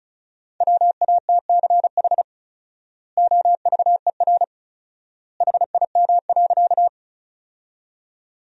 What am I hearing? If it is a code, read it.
Morse code, heard as a keyed tone, about 36 wpm, 700 Hz. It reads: WATCH OVER HIM.
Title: 4x Word-Spacing - Sentences from Top 300 Words - 35wpm